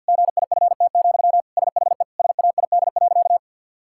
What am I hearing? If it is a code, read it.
Morse code, heard as a keyed tone, about 50 wpm, 700 Hz. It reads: GIFT, SHE SAID.
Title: Sentences from Top 1000 Words - Review Part 6 - 50wpm